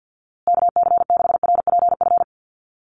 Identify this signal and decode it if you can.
Morse code, heard as a keyed tone, about 48 wpm, 700 Hz. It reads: KC6RPF
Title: USA Call Signs Repeated 3x 50wpm